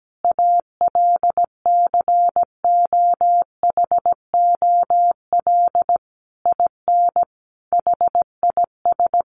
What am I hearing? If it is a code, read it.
Morse code, heard as a keyed tone, about 17 wpm, 700 Hz. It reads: ALCOHOL IN HIS
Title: Sentences from Top 1400 Words 17wpm